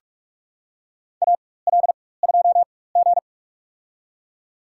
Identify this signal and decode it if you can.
Morse code, heard as a keyed tone, about 45 wpm, spaced as for 21 wpm, 700 Hz. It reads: AL2G
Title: USA Call Signs - 4 Characters - 4x Character Spacing - 45wpm